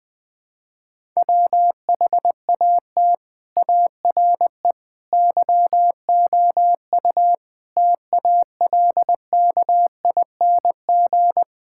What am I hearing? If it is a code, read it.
Morse code, heard as a keyed tone, about 21 wpm, 700 Hz. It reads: WHAT ARE YOU TALKING
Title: Sentences with 5-Word Phrases - Spoken Beginning - 20wpm